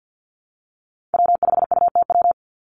Morse code, heard as a keyed tone, about 50 wpm, 700 Hz. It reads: W5UTW
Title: Call Signs Repeated 2x - 50wpm